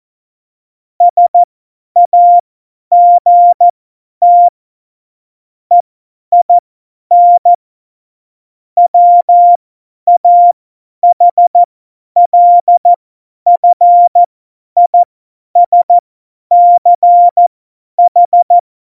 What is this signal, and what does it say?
Morse code, heard as a keyed tone, about 14 wpm, spaced as for 10 wpm, 700 Hz. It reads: SAGT EIN WAHLFISCH
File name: Morsekurs / Witze / Witz 13. November 2018, 50 F70